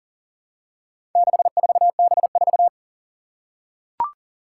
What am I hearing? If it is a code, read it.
Morse code, heard as a keyed tone, about 40 wpm, 700 Hz. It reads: 64B4
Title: Four Challenging Letter-Number Combinations 40wpm